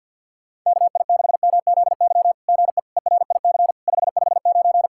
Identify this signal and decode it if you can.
Morse code, heard as a keyed tone, about 50 wpm, 700 Hz. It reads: KI6MZY GE ERIC 559
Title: SOTA QSO Practice 50wpm